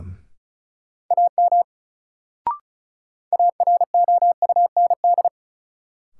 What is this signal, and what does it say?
Morse code, heard as a keyed tone, about 36 wpm, 700 Hz. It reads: AM AROUND
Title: Sentences from Top 600 Words - New Word Drill - 35wpm